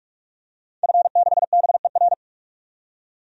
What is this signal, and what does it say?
Morse code, heard as a keyed tone, about 46 wpm, 700 Hz. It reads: F6BER
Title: DXCC-100 Call Signs - 5 Characters - 45wpm